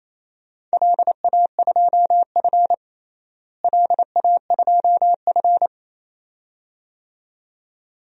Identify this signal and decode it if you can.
Morse code, heard as a keyed tone, about 28 wpm, 700 Hz. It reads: LA2F LA2F
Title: DXCC-100 Call Signs - 4 Characters - Repeated 2x - 28wpm